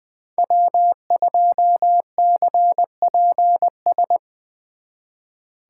Morse code, heard as a keyed tone, about 22 wpm, 700 Hz. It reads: W2CPS
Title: Speed Racing - USA 1x3 Call Signs 20wpm